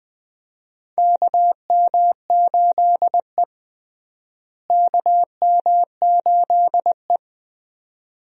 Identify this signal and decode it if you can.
Morse code, heard as a keyed tone, about 20 wpm, 700 Hz. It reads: KM8E KM8E
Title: USA Call Signs - 4 Characters - Repeated 2x - 20wpm